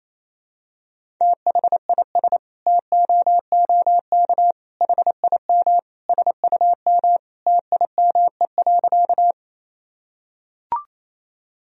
Morse code, heard as a keyed tone, about 28 wpm, 700 Hz. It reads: THIS TOOK HIM SUM TIME.
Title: Sentences from Top 300 Words and Abbreviations 28wpm